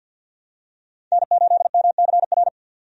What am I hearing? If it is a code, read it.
Morse code, heard as a keyed tone, about 48 wpm, 700 Hz. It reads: N8MCR
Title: USA Call Signs - 5 Characters - 50wpm